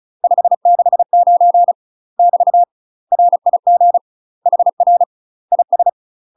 Morse code, heard as a keyed tone, about 36 wpm, 700 Hz. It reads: 569 <BT> RIG HR IS
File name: UK MASTER QSOs TRAINING / 35WPM / UK MASTER QSOs 0013